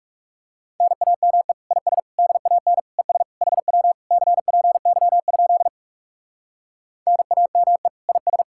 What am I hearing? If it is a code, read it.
Morse code, heard as a keyed tone, about 39 wpm, 700 Hz. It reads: NAME IS DAN ES HW CPY? NAME IS